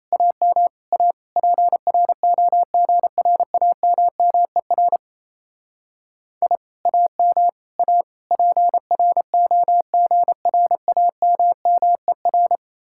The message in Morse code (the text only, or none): AM A PROGRAMMER I AM A PROGRAMMER